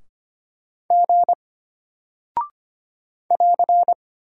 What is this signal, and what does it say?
Morse code, heard as a keyed tone, about 25 wpm, 700 Hz. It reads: G <AR>
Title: Single Character (Letter:Number:Punctuation:Prosign) Alt 25wpm